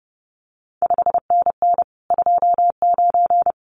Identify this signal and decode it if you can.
Morse code, heard as a keyed tone, about 30 wpm, 700 Hz. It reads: 5NN 29
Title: CQ WW DX Contest Exchange Encore - 30wpm